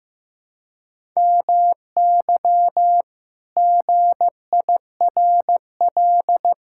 Morse code, heard as a keyed tone, about 15 wpm, 700 Hz. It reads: MY GIRL
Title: Pangram Challenge v2 - 15wpm